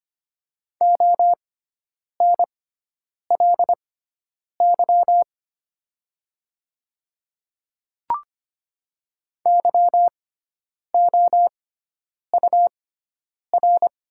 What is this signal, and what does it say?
Morse code, heard as a keyed tone, about 25 wpm, spaced as for 9 wpm, 690 Hz. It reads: ONLY YOUR
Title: 6x Character Spacing - Top 300 Words - 25wpm